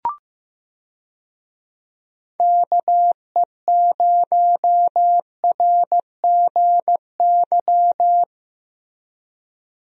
Morse code, heard as a keyed tone, about 15 wpm, 700 Hz. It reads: KE0RGY
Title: USA Call Signs - 6 Characters - Repeated 2x - 15wpm